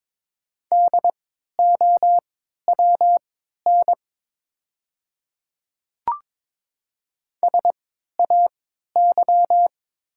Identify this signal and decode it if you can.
Morse code, heard as a keyed tone, about 22 wpm, spaced as for 13 wpm, 700 Hz. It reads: DOWN SAY
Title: 3x Character Spacing - Top 300 Words - 22wpm